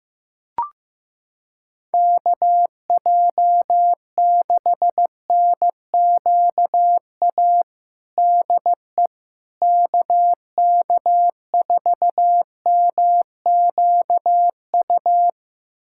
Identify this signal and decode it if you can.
Morse code, heard as a keyed tone, about 15 wpm, 700 Hz. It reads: KJ6NQA DE KK4MQU
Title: USA Callsign DE Callsign 15wpm